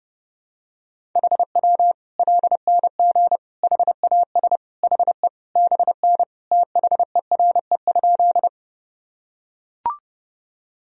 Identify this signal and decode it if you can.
Morse code, heard as a keyed tone, about 30 wpm, 700 Hz. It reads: HW LNG HAS HE BN THERE?
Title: Sentences from Top 100 Words and Abbreviations 30wpm